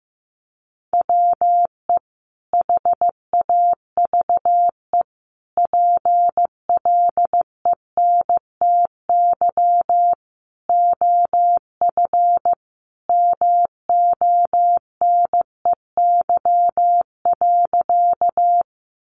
Morse code, heard as a keyed tone, about 15 wpm, 700 Hz. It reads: WE HAVE PLENTY OF MONEY.
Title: Sentences from Top 1300 Words 15wpm